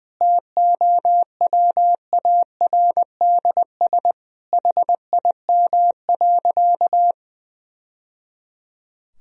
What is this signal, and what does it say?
Morse code, heard as a keyed tone, about 20 wpm, 700 Hz. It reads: TOWARDS HIM.
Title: Sentences from Top 1000 Words - Review Part 2 - 20wpm